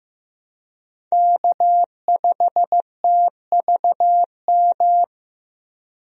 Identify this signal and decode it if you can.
Morse code, heard as a keyed tone, about 15 wpm, 700 Hz. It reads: K5TVM
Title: USA Call Signs - 5 Characters - Repeated 2x - 15wpm